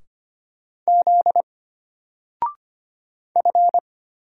Morse code, heard as a keyed tone, about 25 wpm, 700 Hz. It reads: Z F
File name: Single-Letter-Number-25wpm